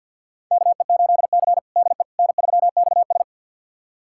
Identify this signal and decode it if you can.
Morse code, heard as a keyed tone, about 45 wpm, 700 Hz. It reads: KE8C DE N3XS
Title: USA Callsign DE Callsign - 4 Characters - 50wpm